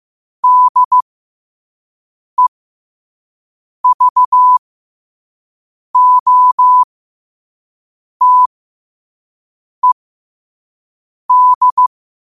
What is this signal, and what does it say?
Morse code, heard as a keyed tone, about 15 wpm, spaced as for 5 wpm, 1000 Hz. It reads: DEVOTED
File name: qotdfiles / 5wpm